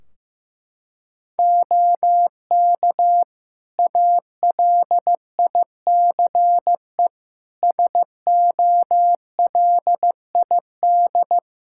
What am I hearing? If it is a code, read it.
Morse code, heard as a keyed tone, about 15 wpm, 700 Hz. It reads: OK ALICE SOLID
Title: Practice QSO - Exchange 3 - More Info - 15wpm